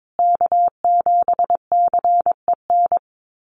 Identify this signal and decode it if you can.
Morse code, heard as a keyed tone, about 22 wpm, 700 Hz. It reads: K7CEN
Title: USA Call Signs Repeated 3x 22wpm